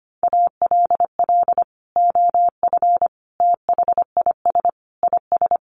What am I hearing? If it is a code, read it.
Morse code, heard as a keyed tone, about 25 wpm, 700 Hz. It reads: ALL OF THIS IS
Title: Sentences from Top 1100 Words 25wpm